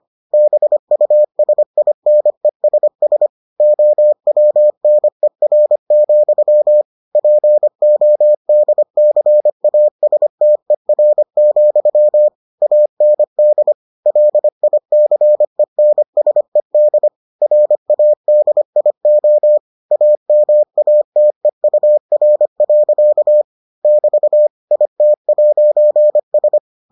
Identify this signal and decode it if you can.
Morse code, heard as a keyed tone, about 25 wpm, 600 Hz. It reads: BUSINESS OWNER, PODCASTER, AND LICENSED RADIO AMATEUR. <BT> IT'S